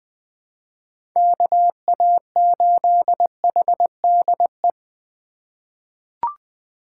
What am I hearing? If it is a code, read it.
Morse code, heard as a keyed tone, about 20 wpm, 700 Hz. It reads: KA8HDE
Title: K1USN SST QSO Practice - 20wpm